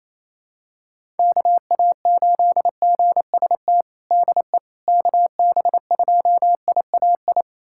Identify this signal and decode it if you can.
Morse code, heard as a keyed tone, about 28 wpm, 700 Hz. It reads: KA8GST DE KB2IAI
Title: USA Callsign DE Callsign - 6 Characters - 28wpm